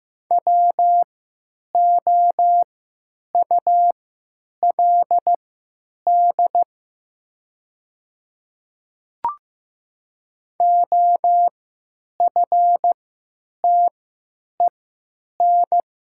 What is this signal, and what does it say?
Morse code, heard as a keyed tone, about 15 wpm, spaced as for 9 wpm, 690 Hz. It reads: WOULD OFTEN
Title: 3x Character Spacing - Top 300 Words - 15wpm